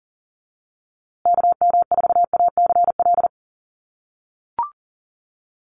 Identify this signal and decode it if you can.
Morse code, heard as a keyed tone, about 40 wpm, 700 Hz. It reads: KM4ACL
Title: USA Call Signs Repeated 3x 40wpm